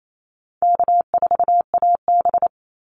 Morse code, heard as a keyed tone, about 27 wpm, 700 Hz. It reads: K4AB
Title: Easy Call Signs - 28wpm